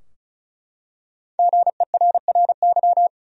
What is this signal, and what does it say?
Morse code, heard as a keyed tone, about 36 wpm, 700 Hz. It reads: GERRY
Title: USA Common Names - 35wpm